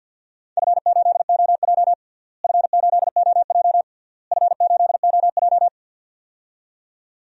Morse code, heard as a keyed tone, about 53 wpm, 700 Hz. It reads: F8OJ F8OJ F8OJ
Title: DXCC-100 Call Signs - 4 Characters - Repeated 3x - 50wpm